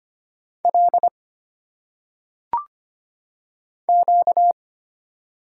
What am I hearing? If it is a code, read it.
Morse code, heard as a keyed tone, about 25 wpm, 700 Hz. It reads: L Q
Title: Single Letters - Phonetic Pronunciation 25wpm